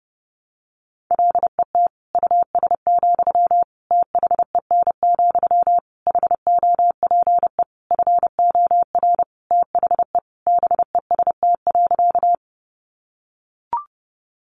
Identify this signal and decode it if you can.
Morse code, heard as a keyed tone, about 30 wpm, 700 Hz. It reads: LET US, THEN, HOPE FOR THE BEST.
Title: Sentences from Top 1000 Words 30wpm